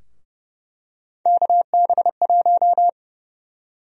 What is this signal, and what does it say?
Morse code, heard as a keyed tone, about 30 wpm, 700 Hz. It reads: KB1
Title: USA Call Sign Prefixes - 30wpm